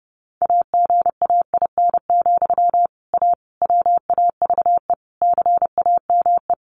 Morse code, heard as a keyed tone, about 30 wpm, 700 Hz. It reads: AGAIN, A WAVE CAME
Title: Sentences from Top 1400 Words 30wpm